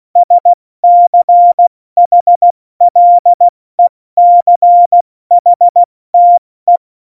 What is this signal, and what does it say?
Morse code, heard as a keyed tone, about 16 wpm, 700 Hz. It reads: SCHLECHTE
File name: Morsekurs / Witze / Witz 22. September 2020, 70 F80